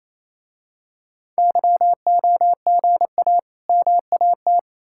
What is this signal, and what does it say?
Morse code, heard as a keyed tone, about 27 wpm, 700 Hz. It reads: YOGA MAT
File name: Easy Words - Speed-Racing with Hints - 22wpm